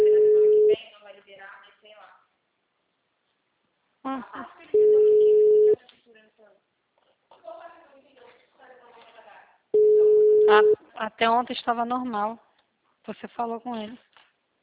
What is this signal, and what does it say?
Ringback tone (Europe, 425 Hz), with speech.
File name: out- -2051- - - .37352